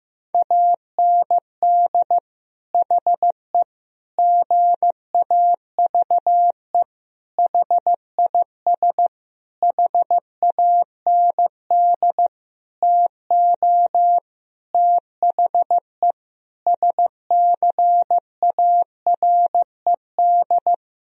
Morse code, heard as a keyed tone, about 15 wpm, 700 Hz. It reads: AND HE GAVE HIS HAND TO THE SCARED
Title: Sentences from Top 1000 Words - Review Part 4 - 15wpm